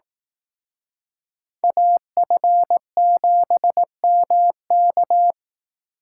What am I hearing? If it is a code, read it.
Morse code, heard as a keyed tone, about 18 wpm, 700 Hz. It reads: AF7MK
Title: Speed Racing - USA Call Signs 15wpm